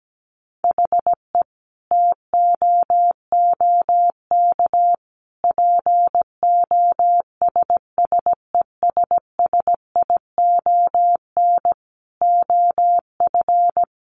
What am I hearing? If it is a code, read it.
Morse code, heard as a keyed tone, about 17 wpm, 700 Hz. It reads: HE TOOK POSSESSION OF
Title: Sentences from Top 1400 Words 17wpm